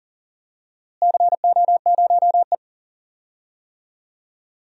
Morse code, heard as a keyed tone, about 40 wpm, 700 Hz. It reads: CO0E
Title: DXCC-100 Call Signs - 4 Characters - Repeated 3x - 40wpm